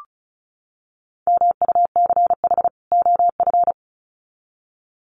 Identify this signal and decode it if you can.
Morse code, heard as a keyed tone, about 34 wpm, 700 Hz. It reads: MUCH OF
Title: Sets of 2 Words 35wpm